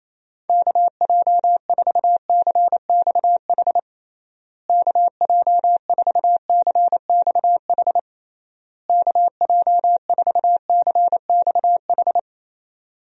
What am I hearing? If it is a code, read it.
Morse code, heard as a keyed tone, about 28 wpm, 700 Hz. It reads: KJ4CXH KJ4CXH KJ4CXH
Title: USA Call Signs - 6 Characters - Repeated 3x - 28wpm